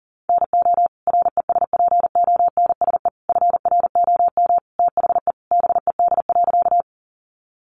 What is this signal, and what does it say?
Morse code, heard as a keyed tone, about 37 wpm, 700 Hz. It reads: NO RESPONSE FROM THE BED.
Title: Sentences from Top 1400 Words 40wpm